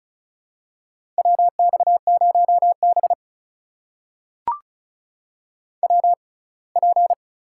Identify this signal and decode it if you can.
Morse code, heard as a keyed tone, about 35 wpm, spaced as for 24 wpm, 700 Hz. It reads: WX0B WP
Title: USA Call Signs - 4 Characters - 6x Character Spacing - 35wpm